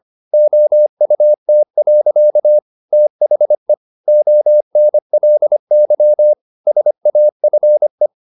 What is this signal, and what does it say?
Morse code, heard as a keyed tone, about 25 wpm, 600 Hz. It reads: OUT. THE ONLY SAFE